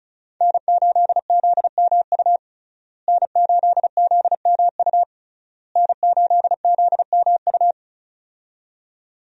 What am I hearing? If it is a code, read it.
Morse code, heard as a keyed tone, about 36 wpm, 700 Hz. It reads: N8ZMU N8ZMU N8ZMU
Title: USA Call Signs - 5 Characters - Repeated 3x - 35wpm